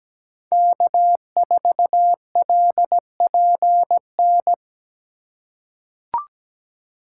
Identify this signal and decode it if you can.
Morse code, heard as a keyed tone, about 17 wpm, 700 Hz. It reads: K4LPN
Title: USA Call Signs - 2x Character Spacing - 17wpm